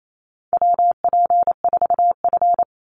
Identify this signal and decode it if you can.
Morse code, heard as a keyed tone, about 28 wpm, 700 Hz. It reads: WP4F
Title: Call Signs Repeated 2x - 28wpm